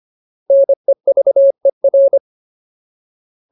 Morse code, heard as a keyed tone, about 25 wpm, 550 Hz. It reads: NEVER
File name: Ultimate Word List / 25WPM Ultimate